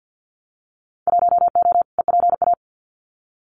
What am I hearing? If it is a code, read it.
Morse code, heard as a keyed tone, about 50 wpm, 700 Hz. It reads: 1O EPA
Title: Winter Field Day Exchanges 50wpm